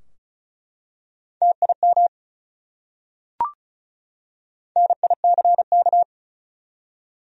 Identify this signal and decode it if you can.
Morse code, heard as a keyed tone, about 36 wpm, 700 Hz. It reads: TIM NICK
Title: USA Common Names - Max 4 Letters - 35wpm